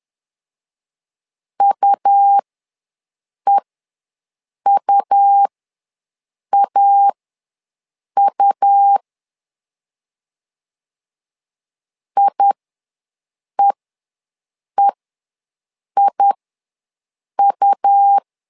Morse code, heard as a keyed tone, about 11 wpm, spaced as for 6 wpm, 800 Hz. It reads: UEUAU IEEIU
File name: LEZIONI / 2 UHDN / 2 U ws 10wpm 2017